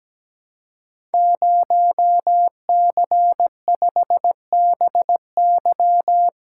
Morse code, heard as a keyed tone, about 17 wpm, 700 Hz. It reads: 0C5BY
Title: Code Groups - 5 Letter:Numbers - 17wpm